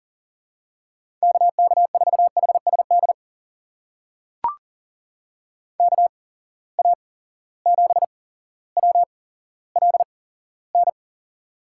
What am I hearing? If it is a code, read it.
Morse code, heard as a keyed tone, about 40 wpm, spaced as for 17 wpm, 700 Hz. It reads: KK4HSD KA7WLN
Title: USA Call Signs - 6 Characters - 8x Character Spacing - 40wpm